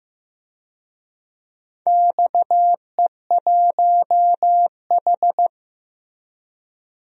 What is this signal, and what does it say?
Morse code, heard as a keyed tone, about 15 wpm, 700 Hz. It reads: XE1H
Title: DXCC-100 Call Signs - 4 Characters - Repeated 3x - 15wpm